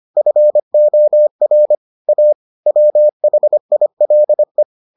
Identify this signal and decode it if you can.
Morse code, heard as a keyed tone, about 25 wpm, 600 Hz. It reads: FOR A WHILE